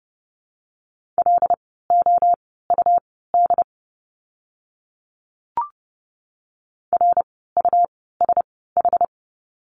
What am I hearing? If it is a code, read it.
Morse code, heard as a keyed tone, about 30 wpm, spaced as for 17 wpm, 700 Hz. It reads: LOUD RUSH
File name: Lesson 17f - Letter u - Words 30wpm